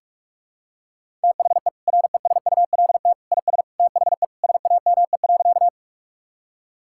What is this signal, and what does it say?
Morse code, heard as a keyed tone, about 46 wpm, 700 Hz. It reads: THE RESULT IS THE SAME.
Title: Sentences from Top 1000 Words - Review Part 5 - 45wpm